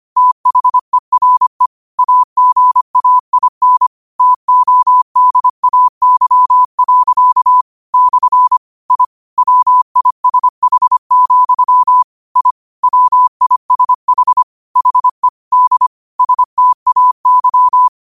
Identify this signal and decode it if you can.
Morse code, heard as a keyed tone, about 25 wpm, 1000 Hz. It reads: THERE AGAIN TODAY. / I WISH, I WISH HE D STAY
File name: qotdfiles / 25wpm